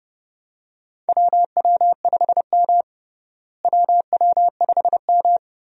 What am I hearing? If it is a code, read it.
Morse code, heard as a keyed tone, about 30 wpm, 700 Hz. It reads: WW5M WW5M
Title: USA Call Signs - 4 Characters - Repeated 2x - 30wpm